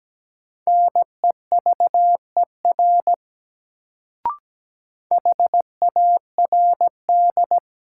Morse code, heard as a keyed tone, about 17 wpm, 700 Hz. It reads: NEVER HARD
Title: Sentences from Top 300 Words - New Word Drill - 17wpm